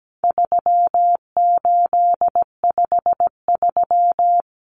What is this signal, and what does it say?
Morse code, heard as a keyed tone, about 17 wpm, 700 Hz. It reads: 3853
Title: Common Cut Numbers - 4-Digit 17wpm